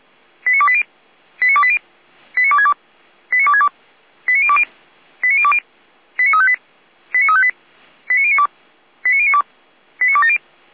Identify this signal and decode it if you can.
Five-tone selective call (ZVEI), heard as quick runs of five tones, tones of about 70 ms. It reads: 89280 89280 89262 89262 89020 89020 89358 89358 89092 89092 89270
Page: SDR/ZVEI